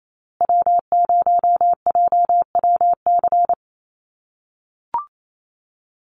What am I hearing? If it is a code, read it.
Morse code, heard as a keyed tone, about 28 wpm, 700 Hz. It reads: W0JWC
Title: USA Call Signs Repeated 2x 28wpm